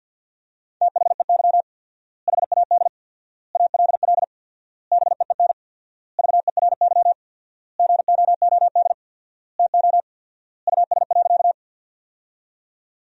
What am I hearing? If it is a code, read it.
Morse code, heard as a keyed tone, about 50 wpm, 700 Hz. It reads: THEY HAD ALL BEEN VERY GOOD TO US.
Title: 4x Word-Spacing - Sentences from Top 200 Words - 50wpm